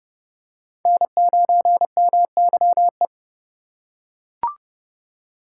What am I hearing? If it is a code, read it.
Morse code, heard as a keyed tone, about 28 wpm, 700 Hz. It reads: N9MYE